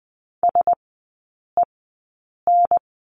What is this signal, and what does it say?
Morse code, heard as a keyed tone, about 20 wpm, spaced as for 8 wpm, 700 Hz. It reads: SEN
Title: 053 Lesson 10e - Letter r - Three Characters 20wpm